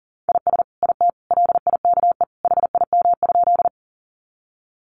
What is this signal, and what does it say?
Morse code, heard as a keyed tone, about 40 wpm, 700 Hz. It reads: IS IT LIKE HIM?
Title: Sentences from Top 100 Words 40wpm